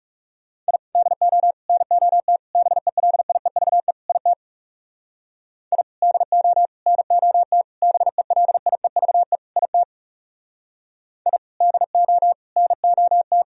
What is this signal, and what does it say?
Morse code, heard as a keyed tone, about 36 wpm, 700 Hz. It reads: I DO NOT BELIEVE IT I DO NOT BELIEVE IT I DO NOT
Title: Speed-Racing - Sets of 5 Words 30wpm